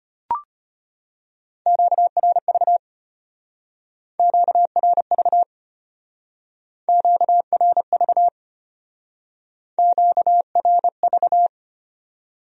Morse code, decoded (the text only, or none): QRV QRV QRV QRV